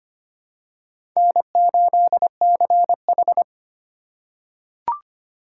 Morse code, heard as a keyed tone, about 25 wpm, 700 Hz. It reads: N8CH
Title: USA Call Signs - 4 Characters - 2x Character Spacing - 25wpm